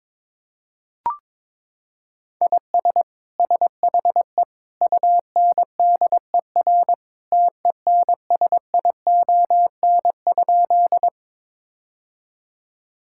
Morse code, heard as a keyed tone, about 22 wpm, 700 Hz. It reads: IS SHE UNDER TENSION?
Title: Sentences from Top 1600 Words 22wpm